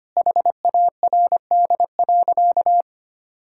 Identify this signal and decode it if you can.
Morse code, heard as a keyed tone, about 25 wpm, 700 Hz. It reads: HARD.
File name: practice / 25wpm / Sentences with 3-Word Phrases - 25wpm